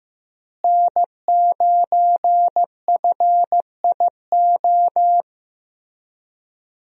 Morse code, heard as a keyed tone, about 15 wpm, 700 Hz. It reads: N9FIO
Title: USA Call Signs - 5 Characters - Repeated 3x - 15wpm